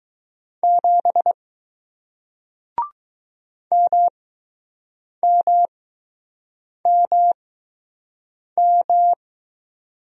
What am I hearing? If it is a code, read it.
Morse code, heard as a keyed tone, about 20 wpm, spaced as for 6 wpm, 700 Hz. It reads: 7 MMMM